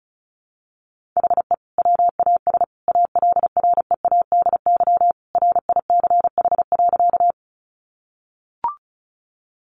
Morse code, heard as a keyed tone, about 34 wpm, 700 Hz. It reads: HE WAS ALREADY RICH.
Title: Sentences from Top 1100 Words 35wpm